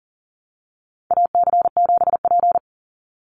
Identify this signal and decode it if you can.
Morse code, heard as a keyed tone, about 40 wpm, 700 Hz. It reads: AC7P
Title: NA SOTA Chaser Callsigns 40wpm